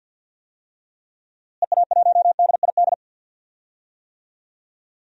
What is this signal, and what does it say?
Morse code, heard as a keyed tone, about 48 wpm, 700 Hz. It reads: EA1DID
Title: DXCC-100 Call Signs - 6 Characters - Repeated 2x - 50wpm